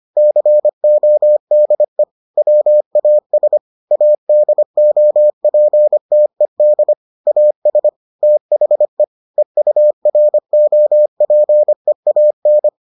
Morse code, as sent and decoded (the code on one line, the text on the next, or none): -.-. --- -.. / .-- .- ... / .- -.. --- .--. - . -.. / .- ... / - .... / . ..- .-. --- .--. .- -.
CODE WAS ADOPTED AS THE EUROPEAN